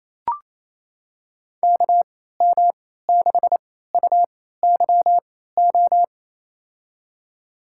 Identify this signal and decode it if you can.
Morse code, heard as a keyed tone, about 28 wpm, spaced as for 16 wpm, 700 Hz. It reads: KM6UYO
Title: USA Call Signs - 6 Characters - 3x Character Spacing - 28wpm